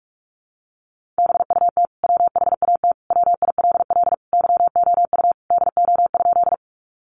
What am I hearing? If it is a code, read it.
Morse code, heard as a keyed tone, about 44 wpm, 700 Hz. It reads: BUT WHAT WILL YOU DO?
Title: Sentences from Top 100 Words 45wpm